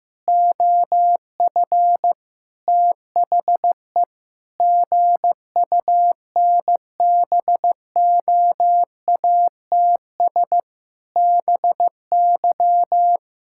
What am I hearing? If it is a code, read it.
Morse code, heard as a keyed tone, about 16 wpm, 700 Hz. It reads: OF THE GUNBOATS BY
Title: Pangram Challenge v2 - 15wpm